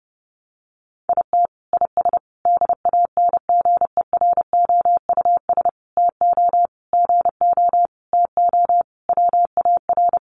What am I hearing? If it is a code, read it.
Morse code, heard as a keyed tone, about 28 wpm, 700 Hz. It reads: IT IS DANGEROUS TO GO TO WAR
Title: Sentences from Top 1200 Words 30wpm